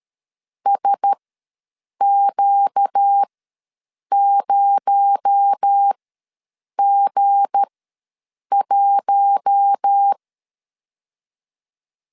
Morse code, heard as a keyed tone, about 13 wpm, spaced as for 7 wpm, 800 Hz. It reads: SQ0G1
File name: LEZIONI / 7 GF9Z / 7 G ws 12wpm 3140